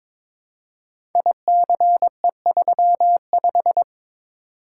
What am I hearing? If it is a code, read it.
Morse code, heard as a keyed tone, about 22 wpm, 700 Hz. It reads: ICE35
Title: Code Groups - 5 Letter:Numbers - 22wpm